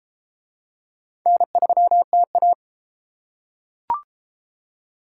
Morse code, heard as a keyed tone, about 33 wpm, 700 Hz. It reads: N3TA